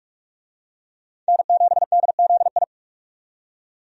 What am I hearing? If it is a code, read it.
Morse code, heard as a keyed tone, about 43 wpm, 700 Hz. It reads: N7DZI
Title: USA Call Signs - 5 Characters - 45wpm